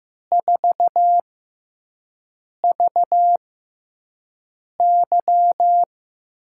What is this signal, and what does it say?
Morse code, heard as a keyed tone, about 15 wpm, spaced as for 5 wpm, 700 Hz. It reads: 4VY
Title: USA Call Signs - 4 Characters - 6x Character Spacing - 15wpm